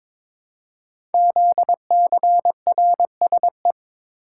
Morse code, heard as a keyed tone, about 22 wpm, 700 Hz. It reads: ZCRSE
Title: Code Groups - 5 Letters - 22wpm